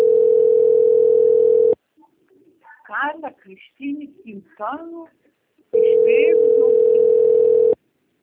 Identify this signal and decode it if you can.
Ringback tone (North America), 440 + 480 Hz, with speech.
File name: rg-1600-unknown- - - .54761